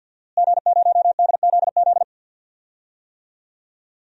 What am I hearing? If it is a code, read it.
Morse code, heard as a keyed tone, about 52 wpm, 700 Hz. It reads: G0DZZ
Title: DXCC-100 Call Signs - 5 Characters - Repeated 3x - 50wpm